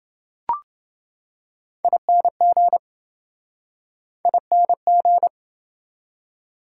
Morse code, heard as a keyed tone, about 28 wpm, 700 Hz. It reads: ING ING